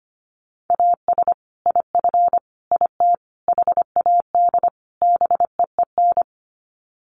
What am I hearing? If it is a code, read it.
Morse code, heard as a keyed tone, about 25 wpm, 700 Hz. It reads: AS IF IT HAD BEEN
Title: Sets of 5 Words Encore2 25wpm